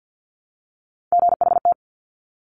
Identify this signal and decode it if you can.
Morse code, heard as a keyed tone, about 50 wpm, 700 Hz. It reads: GHT
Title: Top Three Letter Combinations in Words 50wpm